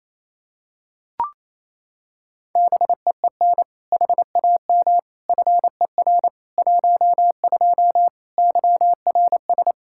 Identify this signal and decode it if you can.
Morse code, heard as a keyed tone, about 28 wpm, 700 Hz. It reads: BEEN HAM FER 12 YRS
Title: Practice QSO - Exchange 3 - More Info - 28wpm